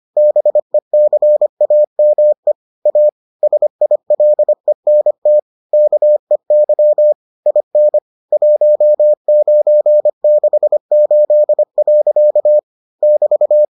Morse code, heard as a keyed tone, about 25 wpm, 600 Hz. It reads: BECAME A SILENT KEY IN 1968. <BT>